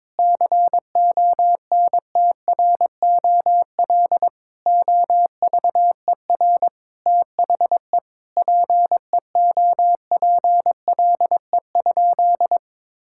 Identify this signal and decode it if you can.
Morse code, heard as a keyed tone, about 21 wpm, 700 Hz. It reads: CONTROL OVER THE PEOPLE?
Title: Sentences from Top 1000 Words - Review Part 5 - 22wpm